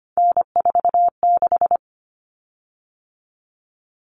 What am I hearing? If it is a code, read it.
Morse code, heard as a keyed tone, about 25 wpm, 700 Hz. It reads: N46
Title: Common Cut Numbers - 3-Digit 25wpm